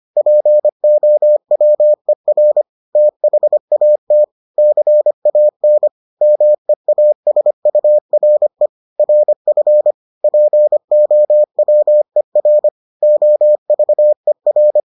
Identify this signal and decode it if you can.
Morse code, heard as a keyed tone, about 25 wpm, 600 Hz. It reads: POWER THAT CAN MEASURE RF POWER OVER